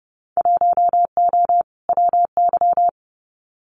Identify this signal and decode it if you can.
Morse code, heard as a keyed tone, about 31 wpm, 700 Hz. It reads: 1O WY
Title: Winter Field Day Exchanges Encore 30wpm